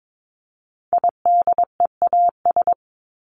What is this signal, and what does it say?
Morse code, heard as a keyed tone, about 22 wpm, 700 Hz. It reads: IDEAS
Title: ES:S Suffix - Top 100 Words with Base Words 22wpm